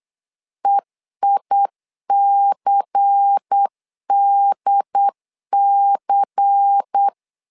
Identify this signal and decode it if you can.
Morse code, heard as a keyed tone, about 8 wpm, 800 Hz. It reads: EICDC
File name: LEZIONI / 5 LBVC / 5 LBVC 8wpm 2935